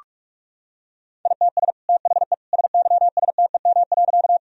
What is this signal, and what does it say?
Morse code, heard as a keyed tone, about 43 wpm, 700 Hz. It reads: ITS THE SYSTEM.
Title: Sentences from Top 1000 Words - Review Part 3 - 45wpm